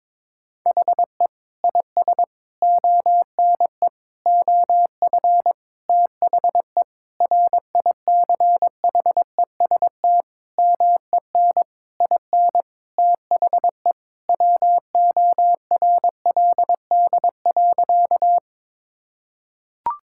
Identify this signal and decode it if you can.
Morse code, heard as a keyed tone, about 21 wpm, 700 Hz. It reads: HE IS ONE OF THE RICHEST MEN IN THE WORLD.
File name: practice / 22wpm / Sentences with 8-Word Phrases - 22wpm